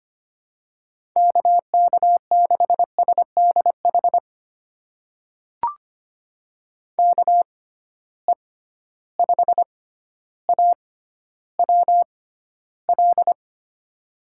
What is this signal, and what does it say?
Morse code, heard as a keyed tone, about 25 wpm, spaced as for 13 wpm, 700 Hz. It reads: KK6SDH KE5AWL
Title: USA Call Signs - 6 Characters - 6x Character Spacing - 25wpm